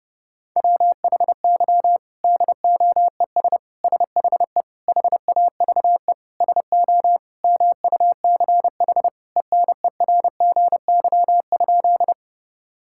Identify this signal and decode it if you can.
Morse code, heard as a keyed tone, about 30 wpm, 700 Hz. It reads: WHY DOES SHE HAVE SO MUCH ENERGY?
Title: Sentences from Top 1000 Words - Review Part 6 - 30wpm